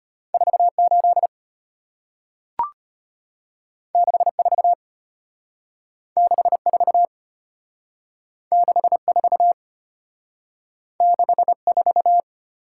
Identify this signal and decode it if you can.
Morse code, heard as a keyed tone, about 33 wpm, 690 Hz. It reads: 48 64 64 64 64